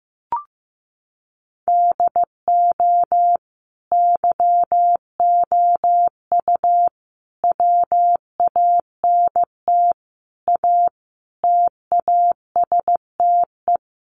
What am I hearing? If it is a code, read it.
Morse code, heard as a keyed tone, about 15 wpm, 700 Hz. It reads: DO YOU WANT A TASTE
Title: Sentences from Top 1100 Words 15wpm